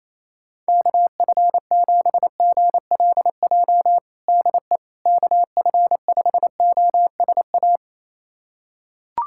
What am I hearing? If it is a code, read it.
Morse code, heard as a keyed tone, about 28 wpm, 700 Hz. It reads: KF7GLJ DE KF5OSA
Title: USA Callsign DE Callsign - 6 Characters - 28wpm